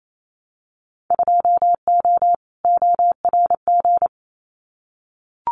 Morse code, heard as a keyed tone, about 28 wpm, 700 Hz. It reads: 2O ORG
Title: Winter Field Day Exchanges 28wpm